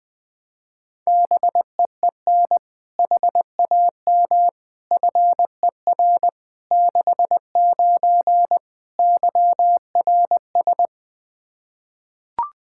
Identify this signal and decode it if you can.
Morse code, heard as a keyed tone, about 20 wpm, 700 Hz. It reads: BEEN HAM FER 69 YRS
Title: Practice QSO - Exchange 3 - More Info - 20wpm